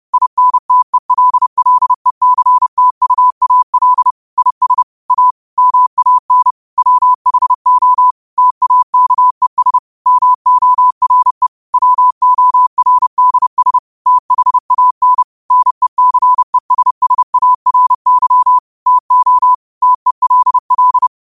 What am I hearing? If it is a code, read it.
Morse code, heard as a keyed tone, about 30 wpm, 1000 Hz. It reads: INTELLECTUAL IS A MAN WHO TAKES MORE WORDS THAN NECESSARY TO TELL